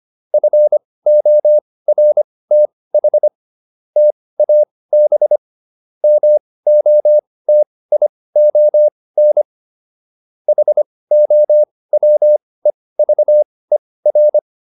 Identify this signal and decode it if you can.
Morse code, heard as a keyed tone, about 25 wpm, spaced as for 18 wpm, 600 Hz. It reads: FORTH TAB MOTION HOWEVER